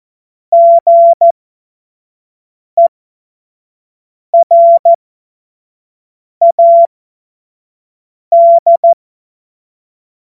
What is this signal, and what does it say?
Morse code, heard as a keyed tone, about 14 wpm, spaced as for 5 wpm, 680 Hz. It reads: GERAD
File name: Morsekurs / Witze / Witz 17. Mai 2022 25 F70